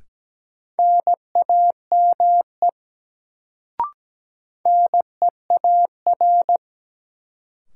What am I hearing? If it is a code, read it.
Morse code, heard as a keyed tone, about 17 wpm, 700 Hz. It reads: NAME NEAR
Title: Sentences from Top 200 Words - New Word Drill - 17wpm